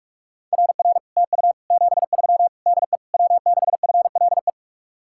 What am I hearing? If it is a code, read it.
Morse code, heard as a keyed tone, about 44 wpm, 700 Hz. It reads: RR TU 73 DE W6FLE